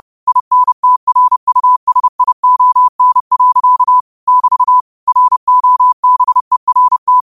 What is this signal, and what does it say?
Morse code, heard as a keyed tone, about 30 wpm, 1000 Hz. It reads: INTRUSION. <BT> ROBERT
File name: qotdfiles / 30wpm